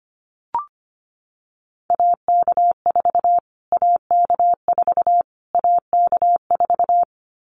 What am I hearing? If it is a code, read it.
Morse code, heard as a keyed tone, about 25 wpm, 700 Hz. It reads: AK4 AK4 AK4
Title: Callsign Prefixes for Top 25 DXCC - Repeated 3x - 25wpm